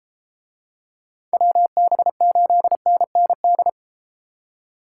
Morse code, heard as a keyed tone, about 32 wpm, 700 Hz. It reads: WB8NND